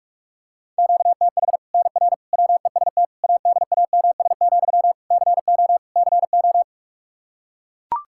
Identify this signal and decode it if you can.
Morse code, heard as a keyed tone, about 41 wpm, 700 Hz. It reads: QTH NR WEST ADAMS, CO CO
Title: Practice QSO - Exchange 1 - Beginning - 45wpm